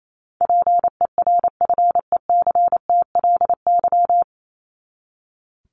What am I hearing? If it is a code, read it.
Morse code, heard as a keyed tone, about 28 wpm, 700 Hz. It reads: PERFECTLY
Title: LY Suffix - Top 100 Words with Base Words 28wpm